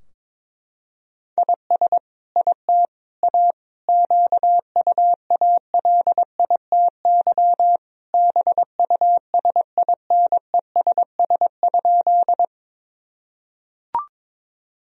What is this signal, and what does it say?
Morse code, heard as a keyed tone, about 22 wpm, 700 Hz. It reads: IS IT A QUALITY BUSINESS?
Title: Sentences from Top 1000 Words - Review Part 4 - 22wpm